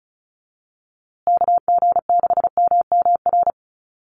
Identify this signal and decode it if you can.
Morse code, heard as a keyed tone, about 34 wpm, 700 Hz. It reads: KG6MMR
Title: USA Call Signs Repeated 3x 35wpm